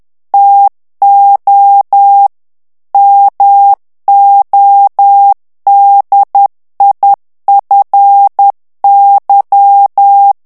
Morse code, heard as a keyed tone, about 10 wpm, 800 Hz. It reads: TO MODIFY
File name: ANGLAIS texte clair 3 vitesse 10 mots minute